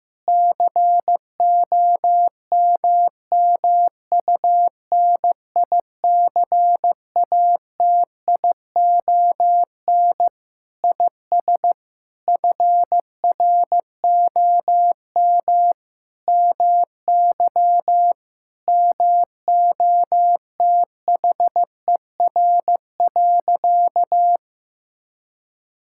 Morse code, heard as a keyed tone, about 16 wpm, 700 Hz. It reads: COMMUNICATION IS FROM MY MOTHER.
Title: Sentences from Top 1000 Words - Review Part 6 - 15wpm